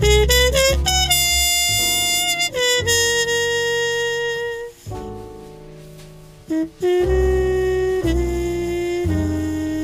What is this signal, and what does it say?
Dial-up modem handshake.